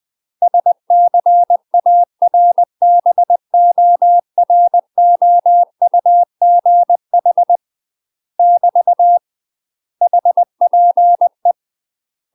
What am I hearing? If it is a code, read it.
Morse code, heard as a keyed tone, about 20 wpm, 700 Hz. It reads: SCARBOROUGH <BT> HPE
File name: UK MASTER QSOs TRAINING / 20WPM / UK MASTER QSOs 0068